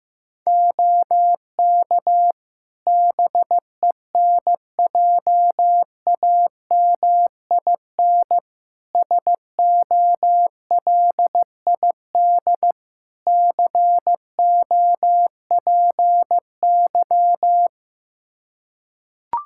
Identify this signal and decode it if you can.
Morse code, heard as a keyed tone, about 15 wpm, 700 Hz. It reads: OK BENJAMIN SOLID COPY
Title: Practice QSO - Exchange 3 - More Info - 15wpm